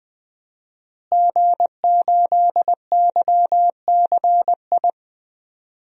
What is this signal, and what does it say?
Morse code, heard as a keyed tone, about 21 wpm, 700 Hz. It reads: G8YCI
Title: DXCC-100 Call Signs - 5 Characters - Repeated 2x - 20wpm